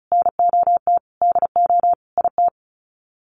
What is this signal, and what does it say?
Morse code, heard as a keyed tone, about 34 wpm, 700 Hz. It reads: NOT DO IT
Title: Sets of 5 Words Encore2 35wpm